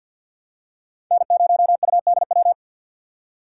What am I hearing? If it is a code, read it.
Morse code, heard as a keyed tone, about 51 wpm, 700 Hz. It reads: N0UDW
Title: USA Call Signs - 5 Characters - 4x Character Spacing - 50wpm